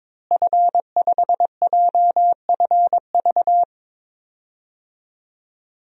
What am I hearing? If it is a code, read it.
Morse code, heard as a keyed tone, about 22 wpm, 700 Hz. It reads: F5JFV
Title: DXCC-100 Call Signs - 5 Characters - Repeated 2x - 22wpm